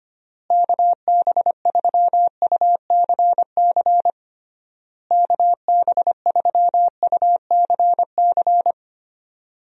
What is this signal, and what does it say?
Morse code, heard as a keyed tone, about 25 wpm, 700 Hz. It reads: KB3UCC KB3UCC
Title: USA Call Signs - 6 Characters - Repeated 3x - 25wpm